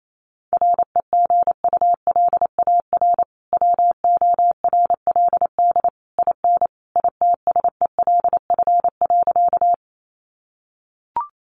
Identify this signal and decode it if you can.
Morse code, heard as a keyed tone, about 28 wpm, 700 Hz. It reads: REGULAR WORLD IN ITSELF.
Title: Sentences from Top 900 Words 28wpm